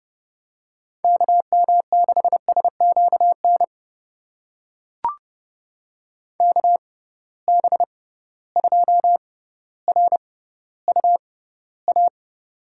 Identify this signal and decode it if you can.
Morse code, heard as a keyed tone, about 32 wpm, spaced as for 16 wpm, 700 Hz. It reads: KM6SQN KB2RUA
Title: USA Call Signs - 6 Characters - 6x Character Spacing - 30wpm